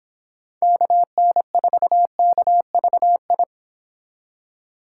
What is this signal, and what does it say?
Morse code, heard as a keyed tone, about 29 wpm, 700 Hz. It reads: KN4KVI